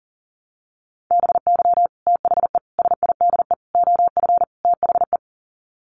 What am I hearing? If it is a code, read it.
Morse code, heard as a keyed tone, about 40 wpm, 700 Hz. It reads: BY THE SIDE OF THE
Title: Sets of 5 Words Encore2 40wpm